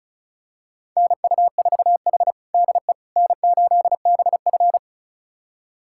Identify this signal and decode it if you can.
Morse code, heard as a keyed tone, about 36 wpm, 700 Hz. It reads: NU4H DE N8BF
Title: USA Callsign DE Callsign - 4 Characters - 35wpm